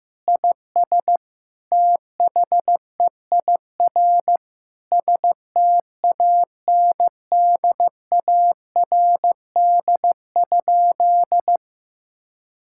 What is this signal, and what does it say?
Morse code, heard as a keyed tone, about 15 wpm, 700 Hz. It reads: IS THEIR STANDARD?
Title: Sentences from Top 1000 Words - Review Part 6 - 15wpm